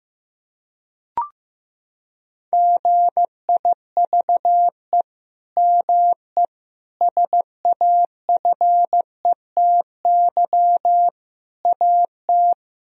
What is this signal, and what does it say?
Morse code, heard as a keyed tone, about 15 wpm, 700 Hz. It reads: GIVE ME SAFETY AT
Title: Sentences from Top 1000 Words - Review Part 4 - 15wpm